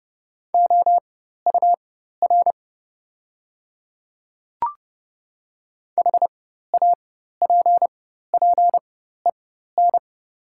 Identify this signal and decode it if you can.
Morse code, heard as a keyed tone, about 30 wpm, spaced as for 14 wpm, 690 Hz. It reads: OUR HAPPEN
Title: 4x Character Spacing - Top 500 Words - 30wpm